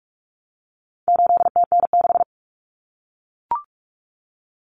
Morse code, heard as a keyed tone, about 44 wpm, 700 Hz. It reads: 8TN6
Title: Common Cut Numbers - 4-Digit 45wpm